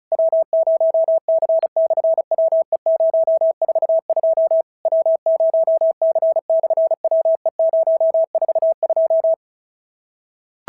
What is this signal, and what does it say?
Morse code, heard as a keyed tone, about 36 wpm, 650 Hz. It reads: W0C/WE042 W0C/WE042
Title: SOTA QSO Practice 35wpm